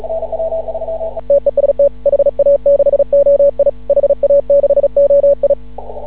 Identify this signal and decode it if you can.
Morse code, heard as a keyed tone, about 36 wpm, 560 Hz. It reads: TEST HA6OI HA6OI
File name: CW / ha6oi 2